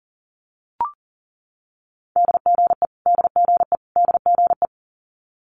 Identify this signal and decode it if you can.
Morse code, heard as a keyed tone, about 40 wpm, 700 Hz. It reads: DGE DGE DGE
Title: Multi-letter Phonemes Repeated 3x 40wpm